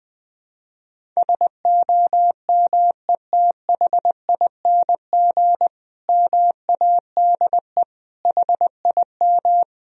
Morse code, heard as a keyed tone, about 20 wpm, 700 Hz. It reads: SOMETHING MADE HIM
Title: Sentences from Top 1000 Words - Review Part 1 - 20wpm